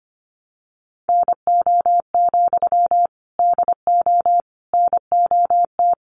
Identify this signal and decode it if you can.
Morse code, heard as a keyed tone, about 25 wpm, 700 Hz. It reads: NO, DO NOT
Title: Sentences from Top 1200 Words 25wpm